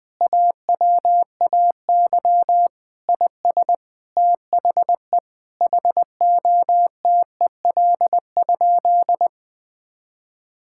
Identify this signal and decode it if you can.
Morse code, heard as a keyed tone, about 21 wpm, 700 Hz. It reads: AWAY IS THE HOTEL?
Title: Sentences from Top 1600 Words 20wpm